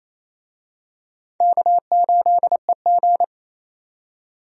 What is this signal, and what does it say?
Morse code, heard as a keyed tone, about 28 wpm, 700 Hz. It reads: K8EG
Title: USA Call Signs - 4 Characters - 28wpm